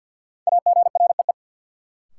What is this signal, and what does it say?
Morse code, heard as a keyed tone, about 48 wpm, 700 Hz. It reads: AGREE
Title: Sets of 5 Words Encore - New Word Drill - 50wpm